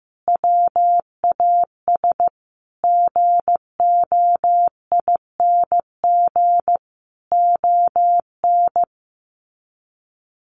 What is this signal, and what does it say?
Morse code, heard as a keyed tone, about 15 wpm, 700 Hz. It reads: WAS GOING ON
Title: Sets of 5 Words Encore2 15wpm